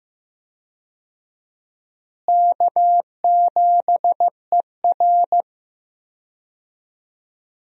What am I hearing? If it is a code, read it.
Morse code, heard as a keyed tone, about 15 wpm, 700 Hz. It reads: K7ER
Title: USA Call Signs - 4 Characters - Repeated 2x - 15wpm